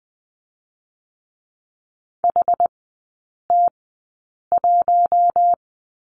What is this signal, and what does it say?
Morse code, heard as a keyed tone, about 20 wpm, spaced as for 8 wpm, 700 Hz. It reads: HT1
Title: 060 Lesson 11e - Letter h - Three Characters 20wpm